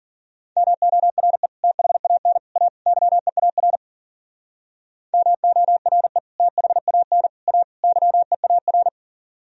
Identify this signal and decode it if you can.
Morse code, heard as a keyed tone, about 50 wpm, 700 Hz. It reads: MORE THAN A YEAR MORE THAN A YEAR
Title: Speed-Racing - Sets of 4 Words 40wpm